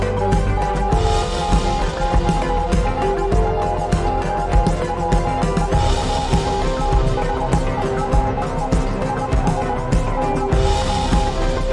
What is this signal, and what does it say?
Morse code, heard as a keyed tone, about 20 wpm, 820 Hz. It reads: TOMORROW. THERE IS NO